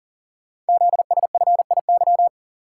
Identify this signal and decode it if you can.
Morse code, heard as a keyed tone, about 40 wpm, 700 Hz. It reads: ZSFIY
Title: Code Groups - 5 Letters - 40wpm